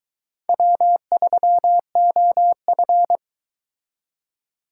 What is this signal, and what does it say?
Morse code, heard as a keyed tone, about 23 wpm, 700 Hz. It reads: W3OF